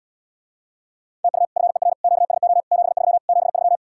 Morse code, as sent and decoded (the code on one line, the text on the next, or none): - ..- / ..-. .-. / --.- ... --- / --... ...-- / --... ...--
TU FER QSO 73 73